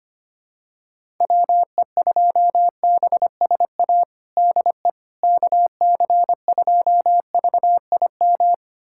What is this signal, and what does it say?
Morse code, heard as a keyed tone, about 25 wpm, 700 Hz. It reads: WE2BSA DE KC2VIM
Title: USA Callsign DE Callsign - 6 Characters - 25wpm